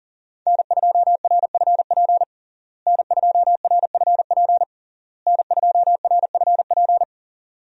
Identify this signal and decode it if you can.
Morse code, heard as a keyed tone, about 37 wpm, 690 Hz. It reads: N2RFP N2RFP N2RFP
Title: USA Call Signs - 5 Characters - Repeated 3x - 40wpm